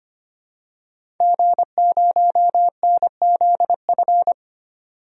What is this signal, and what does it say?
Morse code, heard as a keyed tone, about 25 wpm, 700 Hz. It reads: G0NZF
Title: DXCC-100 Call Signs - 5 Characters - 25wpm